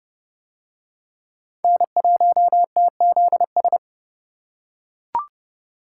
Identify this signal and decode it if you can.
Morse code, heard as a keyed tone, about 26 wpm, 700 Hz. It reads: N1TZS